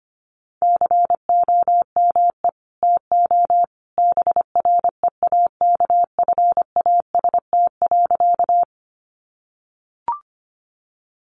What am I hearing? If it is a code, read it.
Morse code, heard as a keyed tone, about 25 wpm, 700 Hz. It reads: COME TO BREAKFAST.
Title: Sentences from Top 1500 Words 25wpm